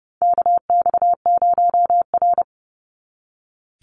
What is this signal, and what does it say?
Morse code, heard as a keyed tone, about 30 wpm, 700 Hz. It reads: KX0R
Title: NA SOTA Activator Callsigns 30wpm